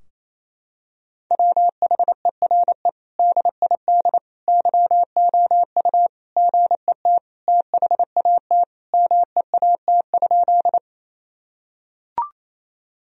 Morse code, heard as a keyed tone, about 27 wpm, 700 Hz. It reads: WHERE DID YOU GET THAT MEAT?
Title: Sentences from Top 1000 Words - Review Part 3 - 28wpm